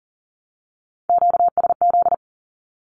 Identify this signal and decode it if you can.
Morse code, heard as a keyed tone, about 40 wpm, 700 Hz. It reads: QSZ
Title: Q-Codes 40wpm